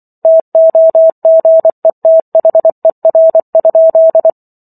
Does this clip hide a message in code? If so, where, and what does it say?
Yes: Morse code from 0.2 to 4.3 s, heard as a keyed tone, about 24 wpm, 650 Hz. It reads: TOGETHER?